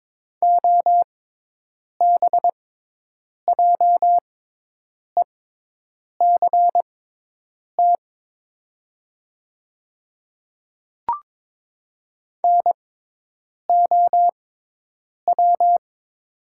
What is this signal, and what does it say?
Morse code, heard as a keyed tone, about 22 wpm, spaced as for 8 wpm, 700 Hz. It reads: OBJECT NOW
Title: 6x Character Spacing - Top 500 Words - 22wpm